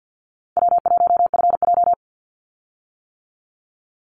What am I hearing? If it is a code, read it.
Morse code, heard as a keyed tone, about 50 wpm, 700 Hz. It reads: W1FJ
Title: Easy Call Signs - 50wpm